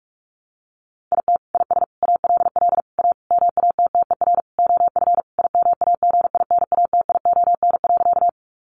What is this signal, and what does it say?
Morse code, heard as a keyed tone, about 44 wpm, 700 Hz. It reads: IT IS ALL A MATTER OF IMAGINATION.
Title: Sentences from Top 1300 Words 45wpm